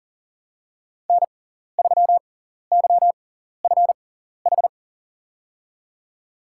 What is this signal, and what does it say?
Morse code, heard as a keyed tone, about 40 wpm, spaced as for 14 wpm, 700 Hz. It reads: N3YFH
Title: USA Call Signs - 6x Character Spacing - 40wpm